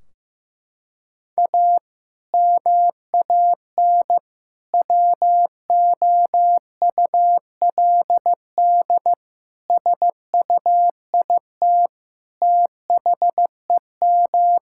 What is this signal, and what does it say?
Morse code, heard as a keyed tone, about 15 wpm, 700 Hz. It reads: A MAN WOULD SUIT THEM
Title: Sentences from Top 1600 Words 15wpm